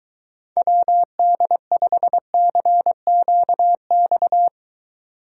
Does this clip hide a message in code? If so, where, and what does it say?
Yes: Morse code from 0.6 to 4.5 s, heard as a keyed tone, about 23 wpm, 700 Hz. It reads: WD5CQX